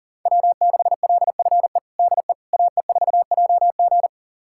Morse code, heard as a keyed tone, about 40 wpm, 700 Hz. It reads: W6LFE DE AE4JG
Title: USA Callsign DE Callsign - 5 Characters - 40wpm